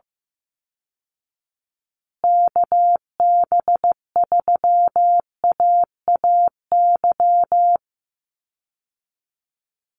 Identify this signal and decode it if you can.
Morse code, heard as a keyed tone, about 15 wpm, 700 Hz. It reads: KB3AAY
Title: Call Signs Repeated 2x - 15wpm